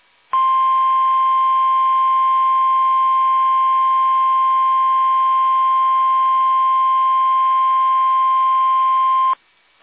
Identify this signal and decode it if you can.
Emergency Alert System (SAME) tones (the 1050 Hz attention tone). It is heard from 0.3 to 9.4 s.
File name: eas / KUSO / KUSO